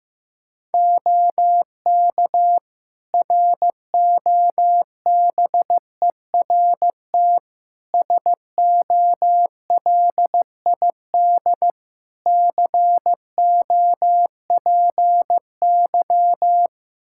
Morse code, heard as a keyed tone, about 15 wpm, 700 Hz. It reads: OK ROBERT SOLID COPY
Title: K1USN SST QSO Practice 15wpm